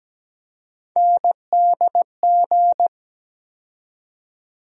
Morse code, heard as a keyed tone, about 17 wpm, 700 Hz. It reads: NDG
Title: USA Call Sign Postfixes - 3 Letters 17wpm